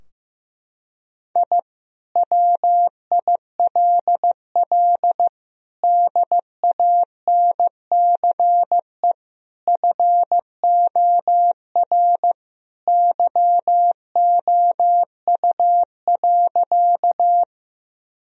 Morse code, heard as a keyed tone, about 15 wpm, 700 Hz. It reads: I WILL DANCE FOR YOU.
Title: Sentences from Top 1000 Words - Review Part 4 - 15wpm